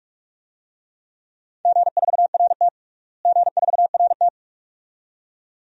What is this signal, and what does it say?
Morse code, heard as a keyed tone, about 43 wpm, 700 Hz. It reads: G4RT G4RT
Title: DXCC-100 Call Signs - 4 Characters - Repeated 2x - 45wpm